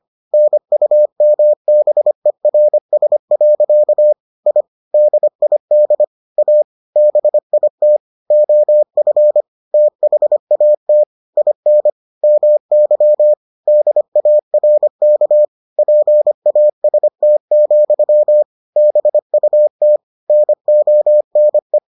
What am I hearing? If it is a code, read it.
Morse code, heard as a keyed tone, about 25 wpm, 600 Hz. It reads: NUMBERS. I DID A BIT OF THAT IN MY DARK PAST, BUT NONE